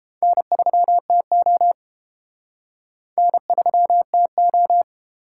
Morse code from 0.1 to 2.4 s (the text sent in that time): N3TO